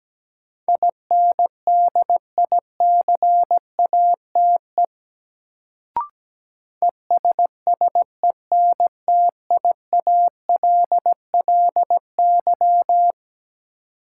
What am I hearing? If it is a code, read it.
Morse code, heard as a keyed tone, about 17 wpm, 700 Hz. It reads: INDICATE ESSENTIALLY
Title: Sentences from Top 1300 Words - New Word Drill - 17wpm